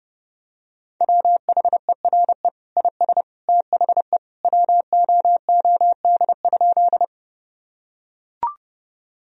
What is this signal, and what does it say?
Morse code, heard as a keyed tone, about 32 wpm, 700 Hz. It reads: WHERE IS THE WOOD?
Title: Sentences from Top 1000 Words - Review Part 2 - 30wpm